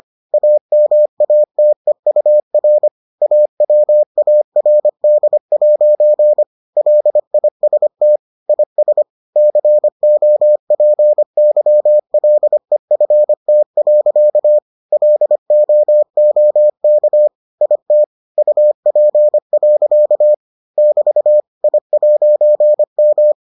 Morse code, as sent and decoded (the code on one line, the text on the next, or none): .- -- .- - . ..- .-. / .- .-- .- .-. -.. .----. / .-.. .. ... - / .. ... / -.-. --- .--. -.-- .-.. ..-. - .-.-.- / .-.. --- --- -.- / .. - / ..- .--. .-.-.- / -...- / .. .----. --
AMATEUR AWARD' LIST IS COPYLEFT. LOOK IT UP. <BT> I'M